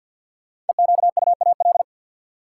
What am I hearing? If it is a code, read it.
Morse code, heard as a keyed tone, about 52 wpm, 700 Hz. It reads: EQUAL
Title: Sentences from Top 1400 Words - New Word Drill - 50wpm